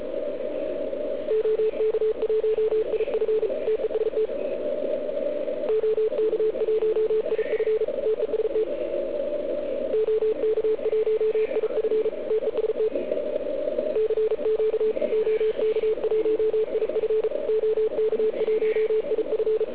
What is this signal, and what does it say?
Morse code, heard as a keyed tone, about 34 wpm, 440 Hz, with radio noise. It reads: OK1IF TEST OK1IF TEST OK1IF TEST CQ OK1IF OK1IF